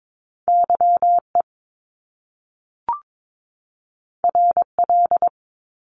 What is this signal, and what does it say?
Morse code, heard as a keyed tone, about 22 wpm, 700 Hz. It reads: Y E R L
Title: Top Two Letter Combinations in Words 22wpm